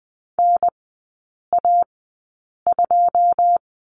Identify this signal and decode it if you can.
Morse code, heard as a keyed tone, about 20 wpm, spaced as for 8 wpm, 690 Hz. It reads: NA2
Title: 081 Lesson 14e - Number 2 - Three Characters 20wpm